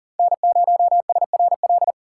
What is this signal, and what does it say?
Morse code, heard as a keyed tone, about 40 wpm, 700 Hz. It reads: N0SRL
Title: USA Call Signs - 5 Characters - 6x Character Spacing - 40wpm